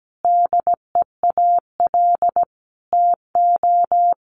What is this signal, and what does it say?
Morse code, heard as a keyed tone, about 17 wpm, 700 Hz. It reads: DEAL TO
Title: Sentences from Top 1000 Words 17wpm